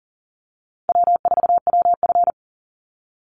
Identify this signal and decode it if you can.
Morse code, heard as a keyed tone, about 38 wpm, 700 Hz. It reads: W4WF
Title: Easy Call Signs Repeated 3x - 40wpm